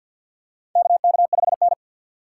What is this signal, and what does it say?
Morse code, heard as a keyed tone, about 51 wpm, 700 Hz. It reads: KK5N
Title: USA Call Signs - 3x Character Spacing - 50wpm